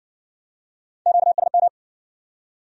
Morse code, heard as a keyed tone, about 60 wpm, 700 Hz. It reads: QSM